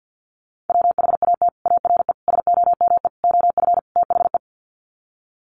Morse code, heard as a keyed tone, about 50 wpm, 700 Hz. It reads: WHAT ARE SOME OF THE
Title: Sets of 5 Words Encore 50wpm